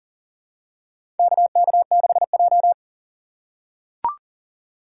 Morse code, heard as a keyed tone, about 40 wpm, 700 Hz. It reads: KK6J
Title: USA Call Signs - 4 Characters - 6x Character Spacing - 40wpm